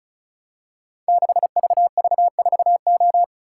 Morse code, heard as a keyed tone, about 36 wpm, 700 Hz. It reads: 6VV4O
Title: Code Groups - 5 Letter:Numbers - 35wpm